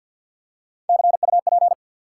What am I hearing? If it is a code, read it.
Morse code, heard as a keyed tone, about 53 wpm, 700 Hz. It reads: CUP
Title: Three Letter Words - 50wpm